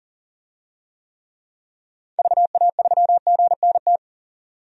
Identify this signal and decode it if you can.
Morse code, heard as a keyed tone, about 40 wpm, 700 Hz. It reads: VA3GNT
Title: DXCC-100 Call Signs - 6 Characters - 40wpm